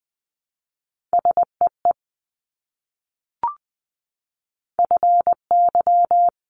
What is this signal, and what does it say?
Morse code, heard as a keyed tone, about 20 wpm, 700 Hz. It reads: SEE FY
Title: Multi-letter Phonemes Repeated 3x 20wpm